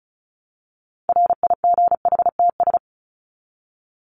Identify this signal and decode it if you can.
Morse code, heard as a keyed tone, about 34 wpm, 700 Hz. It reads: RIGHTS
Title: ES:S Suffix - Top 100 Words with Base Words 35wpm